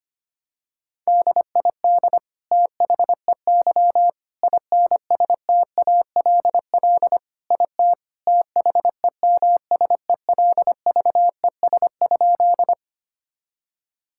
Morse code, heard as a keyed tone, about 25 wpm, 700 Hz. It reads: DID THEY INSTALL IT THEMSELVES?
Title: Sentences from Top 1600 Words 25wpm